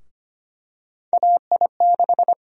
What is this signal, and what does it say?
Morse code, heard as a keyed tone, about 25 wpm, 700 Hz. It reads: AI6
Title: USA Call Sign Prefixes - Repeated 2x - 25wpm